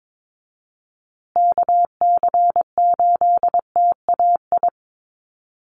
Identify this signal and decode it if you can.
Morse code, heard as a keyed tone, about 22 wpm, 700 Hz. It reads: KC8TAI
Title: USA Call Signs Repeated 3x 22wpm